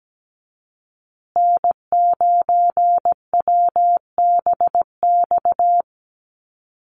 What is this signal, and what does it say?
Morse code, heard as a keyed tone, about 17 wpm, 700 Hz. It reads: N9WBX
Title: USA Call Signs Repeated 2x 17wpm